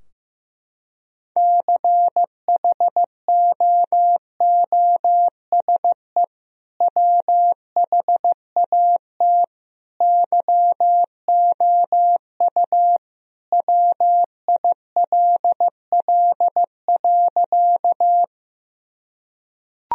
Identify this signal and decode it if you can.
Morse code, heard as a keyed tone, about 15 wpm, 700 Hz. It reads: CHOOSE WHAT YOU WILL.
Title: Sentences from Top 1000 Words - Review Part 4 - 15wpm